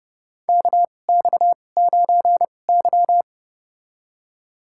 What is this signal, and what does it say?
Morse code, heard as a keyed tone, about 30 wpm, spaced as for 22 wpm, 700 Hz. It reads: KX9Y
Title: USA Call Signs - 4 Characters - 2x Character Spacing - 30wpm